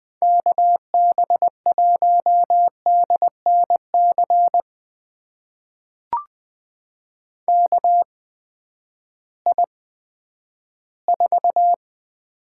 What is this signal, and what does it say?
Morse code, heard as a keyed tone, about 20 wpm, spaced as for 11 wpm, 700 Hz. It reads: KB1DNC KI4
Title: USA Call Signs - 6 Characters - 8x Character Spacing - 20wpm